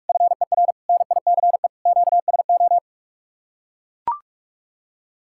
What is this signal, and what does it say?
Morse code, heard as a keyed tone, about 44 wpm, 700 Hz. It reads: FER NICE QSO